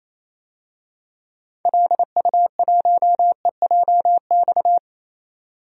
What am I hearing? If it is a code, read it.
Morse code, heard as a keyed tone, about 27 wpm, 700 Hz. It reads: LU1EJX
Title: DXCC-100 Call Signs - 6 Characters - 28wpm